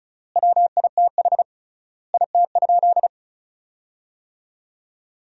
Morse code, heard as a keyed tone, about 34 wpm, 700 Hz. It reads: WITH IT?
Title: 3x Word-Spacing - Sentences from Top 100 Words - 35wpm